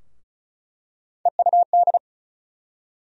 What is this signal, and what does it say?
Morse code, heard as a keyed tone, about 34 wpm, 700 Hz. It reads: EUD
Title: USA Call Sign Postfixes - 3 Letters 35wpm